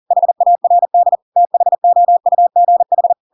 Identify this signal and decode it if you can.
Morse code, heard as a keyed tone, about 40 wpm, 700 Hz. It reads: HARD THOUGH
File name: Pinocchio / Pinocchio ch5 40 40 ik1hge